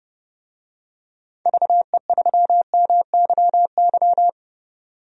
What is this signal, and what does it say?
Morse code, heard as a keyed tone, about 30 wpm, 700 Hz. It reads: VE3MYY
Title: DXCC-100 Call Signs - 6 Characters - 30wpm